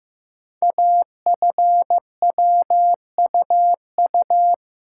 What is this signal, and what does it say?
Morse code, heard as a keyed tone, about 15 wpm, 700 Hz. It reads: AFWUU
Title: Code Groups - 5 Letters - 15wpm